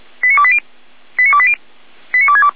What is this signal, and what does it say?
Five-tone selective call (ZVEI), heard as quick runs of five tones, tones of about 70 ms. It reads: 89280 89280 89262